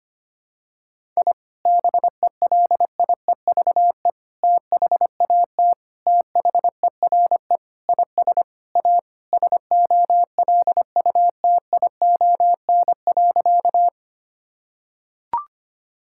Morse code, heard as a keyed tone, about 25 wpm, 700 Hz. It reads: I BELIEVE THAT THERE IS A SOLUTION.
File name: practice / 25wpm / Sentences with 4-Word Phrases - 25wpm